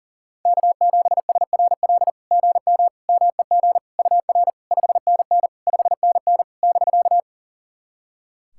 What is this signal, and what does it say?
Morse code, heard as a keyed tone, about 44 wpm, 700 Hz. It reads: K7SRL GM MEG UR 5NN 5NN <BK>
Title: SOTA QSO Practice 40wpm